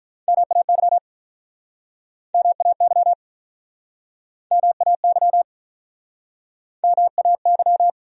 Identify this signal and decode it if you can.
Morse code, heard as a keyed tone, about 45 wpm, 700 Hz. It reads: MAY MAY MAY MAY